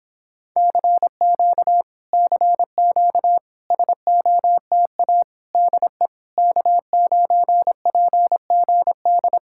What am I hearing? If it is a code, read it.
Morse code, heard as a keyed tone, about 26 wpm, 700 Hz. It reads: CQ CQ SOTA DE K9PGD